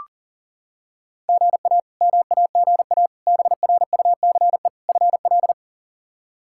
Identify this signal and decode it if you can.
Morse code, heard as a keyed tone, about 40 wpm, 700 Hz. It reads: GA MAGA BRUCE FL
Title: K1USN SST QSO Practice - 40wpm